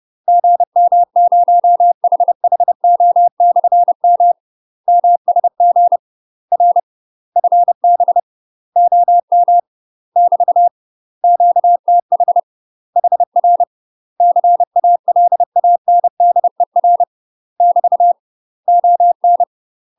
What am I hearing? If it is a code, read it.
Morse code, heard as a keyed tone, about 30 wpm, 700 Hz. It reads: GM0HHO/M MSG R FB OM <BT> QTH HR CALANDER <BT> ON
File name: UK MASTER QSOs TRAINING / 30WPM / UK MASTER QSOs 0016